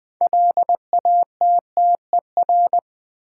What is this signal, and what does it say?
Morse code, heard as a keyed tone, about 20 wpm, 700 Hz. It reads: LATTER
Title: Sentences from Top 1600 Words - New Word Drill - 20wpm